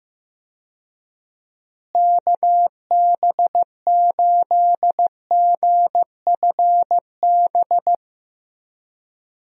Morse code, heard as a keyed tone, about 15 wpm, 700 Hz. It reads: KB8GFB
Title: USA Call Signs - 6 Characters - Repeated 2x - 15wpm